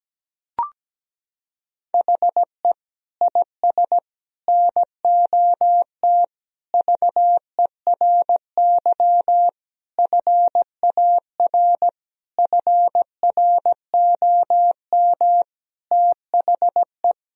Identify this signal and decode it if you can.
Morse code, heard as a keyed tone, about 17 wpm, 700 Hz. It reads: HE IS NOT VERY FAR FROM THE
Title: Sentences from Top 1000 Words - Review Part 1 - 17wpm